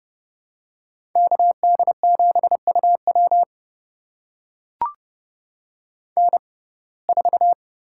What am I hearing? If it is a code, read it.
Morse code, heard as a keyed tone, about 30 wpm, spaced as for 22 wpm, 700 Hz. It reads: KD7UW N4
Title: USA Call Signs - 5 Characters - 6x Character Spacing - 30wpm